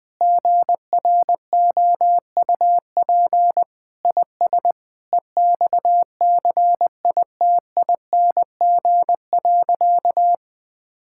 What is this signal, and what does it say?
Morse code, heard as a keyed tone, about 21 wpm, 700 Hz. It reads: GROUP IS EXCITING.
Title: Sentences with 4-Word Phrases - 20wpm